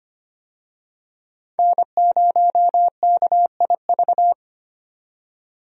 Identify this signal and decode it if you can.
Morse code, heard as a keyed tone, about 25 wpm, 700 Hz. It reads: N0KIV
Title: USA Call Signs - 5 Characters - 25wpm